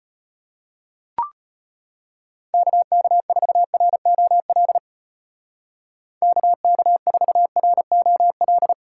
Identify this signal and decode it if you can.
Morse code, heard as a keyed tone, about 31 wpm, 700 Hz. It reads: KK4ROL KK4ROL
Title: Speed Racing - USA 2x3 Call Signs 25wpm